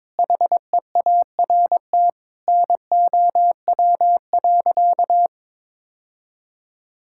Morse code, heard as a keyed tone, about 21 wpm, 700 Hz. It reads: HEART NOW.
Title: Sentences from Top 1600 Words 22wpm